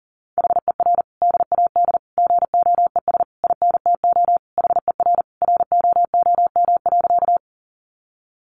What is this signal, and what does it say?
Morse code, heard as a keyed tone, about 40 wpm, 700 Hz. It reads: HER DAD GOES INTO HER ROOM.
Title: Additional Easy Phrases 40wpm